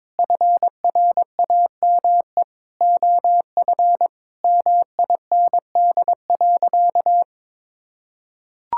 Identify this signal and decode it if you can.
Morse code, heard as a keyed tone, about 22 wpm, 700 Hz. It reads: FRAME OF MIND.
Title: Sentences from Top 1000 Words - Review Part 6 - 22wpm